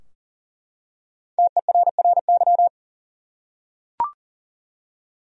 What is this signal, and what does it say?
Morse code, heard as a keyed tone, about 40 wpm, 700 Hz. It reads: TERRY
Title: USA Common Names - 40wpm